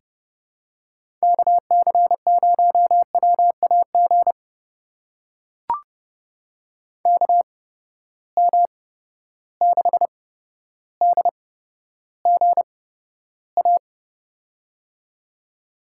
Morse code, heard as a keyed tone, about 30 wpm, spaced as for 13 wpm, 700 Hz. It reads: KC0WAG KM6DGA
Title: USA Call Signs - 6 Characters - 8x Character Spacing - 30wpm